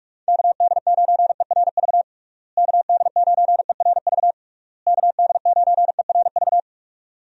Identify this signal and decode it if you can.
Morse code, heard as a keyed tone, about 46 wpm, 700 Hz. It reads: KD9ERV KD9ERV KD9ERV
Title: USA Call Signs - 6 Characters - Repeated 3x - 45wpm